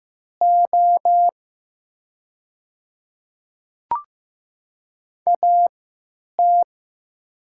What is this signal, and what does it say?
Morse code, heard as a keyed tone, about 15 wpm, spaced as for 9 wpm, 700 Hz. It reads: O AT
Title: 3x Character Spacing - Top 100 Words - 15wpm